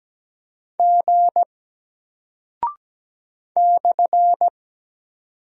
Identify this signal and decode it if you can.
Morse code, heard as a keyed tone, about 17 wpm, 700 Hz. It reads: G /
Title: Single Character (Letter:Number:Punctuation:Prosign) 17wpm